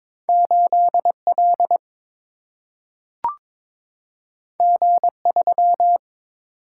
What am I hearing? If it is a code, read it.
Morse code, heard as a keyed tone, about 22 wpm, 700 Hz. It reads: 8L G3
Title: Single-Letter Single-Number 22wpm